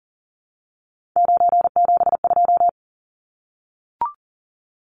Morse code, heard as a keyed tone, about 40 wpm, 700 Hz. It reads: 972
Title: Numbers - 3-Digit 40wpm